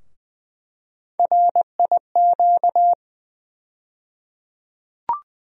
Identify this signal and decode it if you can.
Morse code, heard as a keyed tone, about 20 wpm, 700 Hz. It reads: RIQ
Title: USA Call Sign Postfixes - 3 Letters 20wpm